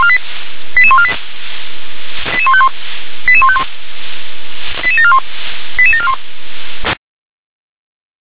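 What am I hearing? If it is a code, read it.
Five-tone selective call (ZVEI), heard as quick runs of five tones, tones of about 70 ms. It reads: 88148 88151 88151 88741 88741
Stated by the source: Five-tone ZVEI selective call received on SDR.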